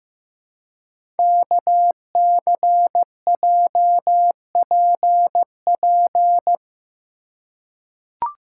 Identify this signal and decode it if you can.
Morse code, heard as a keyed tone, about 15 wpm, 700 Hz. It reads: KCJPP
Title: Code Groups - 5 Letters - 15wpm